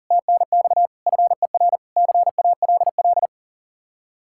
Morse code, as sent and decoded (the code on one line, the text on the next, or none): - -. -..- / ..-. .-. / -.-. .- .-.. .-..
TNX FER CALL